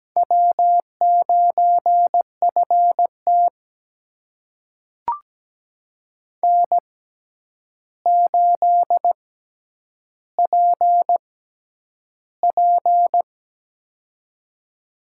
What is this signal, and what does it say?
Morse code, heard as a keyed tone, about 17 wpm, spaced as for 9 wpm, 700 Hz. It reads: W9FT N8PP
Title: USA Call Signs - 4 Characters - 6x Character Spacing - 17wpm